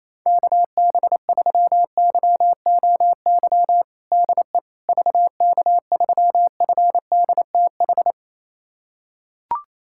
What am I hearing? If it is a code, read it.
Morse code, heard as a keyed tone, about 28 wpm, 700 Hz. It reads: KB3YOY DE VK3FDTH
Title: USA Callsign DE Callsign 28wpm